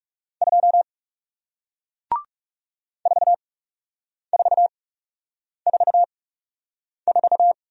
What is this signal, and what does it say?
Morse code, heard as a keyed tone, about 40 wpm, spaced as for 8 wpm, 700 Hz. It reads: 2 4444